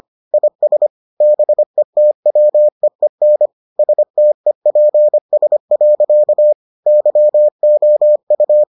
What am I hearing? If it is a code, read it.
Morse code, heard as a keyed tone, about 25 wpm, 600 Hz. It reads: IS BETWEEN STEPS. YOU